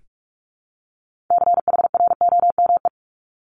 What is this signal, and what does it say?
Morse code, heard as a keyed tone, about 46 wpm, 700 Hz. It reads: CHROME
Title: Multi-letter Phonemes 45wpm